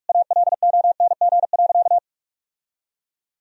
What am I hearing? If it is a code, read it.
Morse code, heard as a keyed tone, about 42 wpm, 700 Hz. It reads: ALONG.
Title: Sentences from Top 1000 Words - Review Part 3 - 45wpm